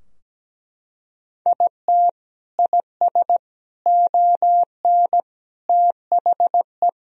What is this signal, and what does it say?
Morse code, heard as a keyed tone, about 17 wpm, 700 Hz. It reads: IT IS ON THE
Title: Sentences from Top 1000 Words - Review Part 2 - 17wpm